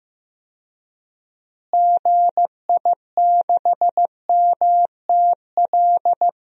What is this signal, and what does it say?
Morse code, heard as a keyed tone, about 15 wpm, 700 Hz. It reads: GI6MTL
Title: DXCC-100 Call Signs - 6 Characters - Repeated 3x - 15wpm